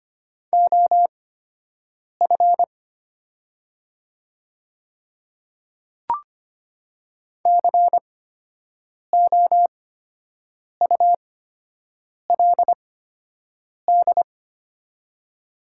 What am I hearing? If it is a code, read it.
Morse code, heard as a keyed tone, about 25 wpm, spaced as for 7 wpm, 700 Hz. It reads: OF COULD
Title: 8x Character Spacing - Top 100 Words - 25wpm